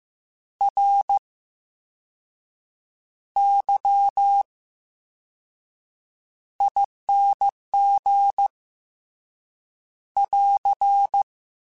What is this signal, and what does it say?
Morse code, heard as a keyed tone, about 15 wpm, spaced as for 6 wpm, 800 Hz. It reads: RYING <AR>
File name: Special INGs / Special INGs-mp3s / Slow ing @13